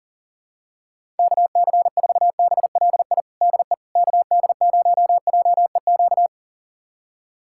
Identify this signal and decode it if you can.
Morse code, heard as a keyed tone, about 40 wpm, 700 Hz. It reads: KC4BLI DE KD0JEQ
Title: USA Callsign DE Callsign 40wpm